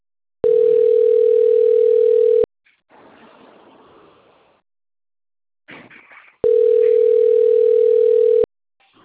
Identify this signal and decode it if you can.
Ringback tone (North America), 440 + 480 Hz.